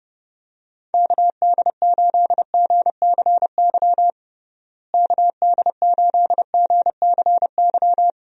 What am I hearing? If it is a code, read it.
Morse code, heard as a keyed tone, about 30 wpm, 700 Hz. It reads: KD8GCY KD8GCY
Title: USA Call Signs - 6 Characters - Repeated 3x - 30wpm